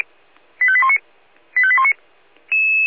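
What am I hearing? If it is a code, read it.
Five-tone selective call (ZVEI), heard as quick runs of five tones, tones of about 70 ms. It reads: 85819 85819
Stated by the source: ZVEI five-tone call 85857 received by SDR.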